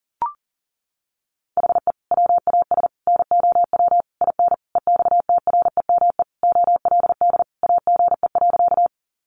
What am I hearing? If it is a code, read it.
Morse code, heard as a keyed tone, about 37 wpm, 700 Hz. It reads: HE WAS NOW IN EXTREME OLD AGE.
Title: Sentences from Top 1300 Words 40wpm